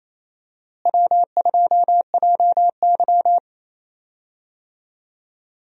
Morse code, heard as a keyed tone, about 27 wpm, 700 Hz. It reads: W2JY
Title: USA Call Signs - 4 Characters - Repeated 3x - 28wpm